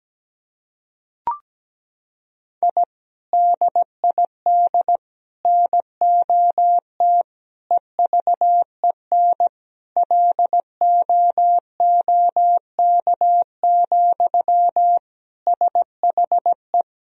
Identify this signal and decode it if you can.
Morse code, heard as a keyed tone, about 17 wpm, 700 Hz. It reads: I DID NOT EVEN LOOK, SHE
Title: Sentences from Top 1000 Words - Review Part 1 - 17wpm